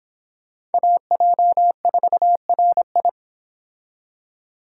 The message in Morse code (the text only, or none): AJ4RI